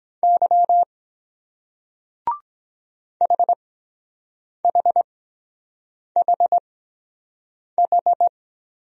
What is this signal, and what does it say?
Morse code, heard as a keyed tone, about 23 wpm, spaced as for 7 wpm, 700 Hz. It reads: Y HHHH